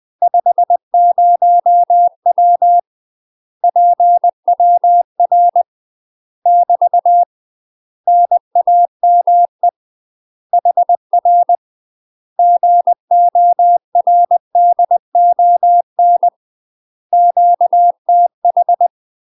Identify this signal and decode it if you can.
Morse code, heard as a keyed tone, about 20 wpm, 700 Hz. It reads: 50W PWR <BT> NAME HR GORDON QTH
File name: UK MASTER QSOs TRAINING / 20WPM / UK MASTER QSOs 0060